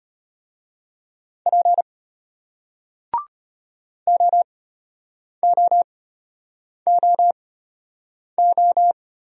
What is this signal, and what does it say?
Morse code, heard as a keyed tone, about 33 wpm, spaced as for 8 wpm, 700 Hz. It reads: P OOOO